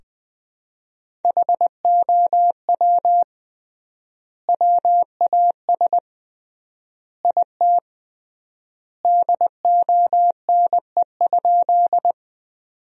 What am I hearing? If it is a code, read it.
Morse code, heard as a keyed tone, about 20 wpm, 700 Hz. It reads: HOW WAS IT DONE?
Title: 3x Word-Spacing - Sentences from Top 500 Words - 20wpm